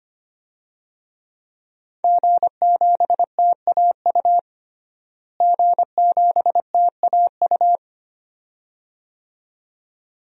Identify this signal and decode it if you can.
Morse code, heard as a keyed tone, about 25 wpm, 700 Hz. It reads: G7TAU G7TAU
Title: DXCC-100 Call Signs - 5 Characters - Repeated 2x - 25wpm